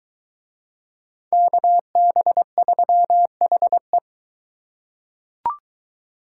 Morse code, heard as a keyed tone, about 22 wpm, 700 Hz. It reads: KB3HE